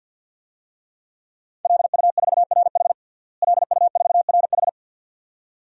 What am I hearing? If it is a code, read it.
Morse code, heard as a keyed tone, about 52 wpm, 700 Hz. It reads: LU4RH LU4RH
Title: DXCC-100 Call Signs - 5 Characters - Repeated 2x - 50wpm